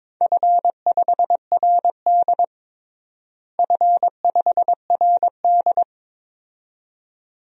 Morse code, heard as a keyed tone, about 22 wpm, 700 Hz. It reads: F5RD F5RD
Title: DXCC-100 Call Signs - 4 Characters - Repeated 2x - 22wpm